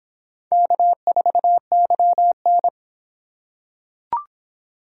Morse code, heard as a keyed tone, about 25 wpm, 700 Hz. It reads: K4YN